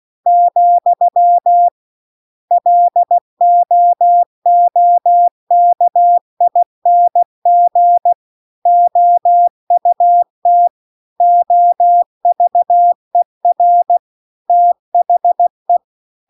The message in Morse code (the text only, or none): , LOOKING OUT OVER THE